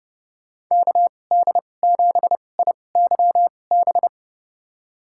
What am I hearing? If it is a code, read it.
Morse code, heard as a keyed tone, about 30 wpm, spaced as for 22 wpm, 700 Hz. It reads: KD7IYB
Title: USA Call Signs - 6 Characters - 2x Character Spacing - 30wpm